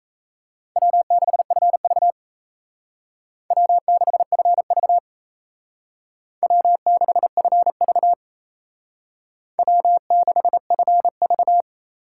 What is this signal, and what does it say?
Morse code, heard as a keyed tone, about 36 wpm, 700 Hz. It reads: W6FV W6FV W6FV W6FV